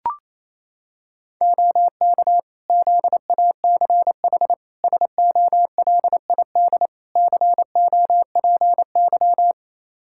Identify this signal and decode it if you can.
Morse code, heard as a keyed tone, about 27 wpm, 700 Hz. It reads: OK ZACH SOLID COPY
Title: K1USN SST QSO Practice 28wpm